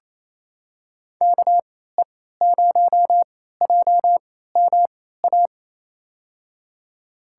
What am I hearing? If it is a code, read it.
Morse code, heard as a keyed tone, about 28 wpm, spaced as for 16 wpm, 700 Hz. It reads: KE0JMA
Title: USA Call Signs - 6 Characters - 3x Character Spacing - 28wpm